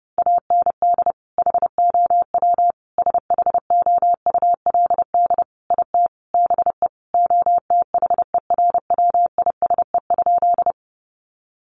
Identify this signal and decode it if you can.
Morse code, heard as a keyed tone, about 30 wpm, 700 Hz. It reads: AND HOW SHOULD IT BE OTHERWISE?
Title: Sentences from Top 1200 Words 30wpm